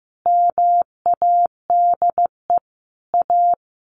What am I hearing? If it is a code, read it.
Morse code, heard as a keyed tone, about 14 wpm, 700 Hz. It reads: MADE A
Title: Sentences from Top 1200 Words 15wpm